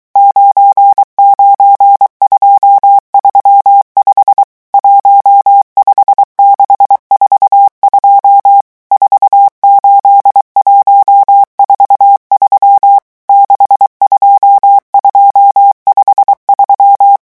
Morse code, heard as a keyed tone, about 23 wpm, 800 Hz. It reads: 99235 15642 48143 62253
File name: morse mp3 / revision11 chiffres vitesse 22